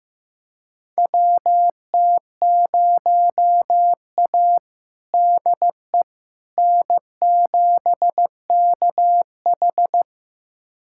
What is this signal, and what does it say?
Morse code, heard as a keyed tone, about 15 wpm, 700 Hz. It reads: WT0A DE N7KH
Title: USA Callsign DE Callsign - 4 Characters - 15wpm